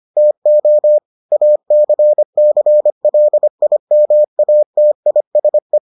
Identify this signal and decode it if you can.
Morse code, heard as a keyed tone, about 25 wpm, 600 Hz. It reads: TO ACCLIMATISE